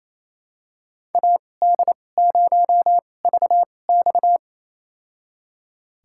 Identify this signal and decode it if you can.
Morse code, heard as a keyed tone, about 28 wpm, spaced as for 20 wpm, 700 Hz. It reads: AD0VX
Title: USA Call Signs - 5 Characters - 2x Character Spacing - 28wpm